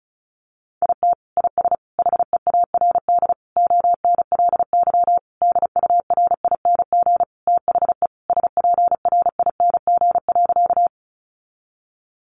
Morse code, heard as a keyed tone, about 34 wpm, 700 Hz. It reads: IT IS HEARD ONLY DURING THE SPRING.
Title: Sentences from Top 1200 Words 35wpm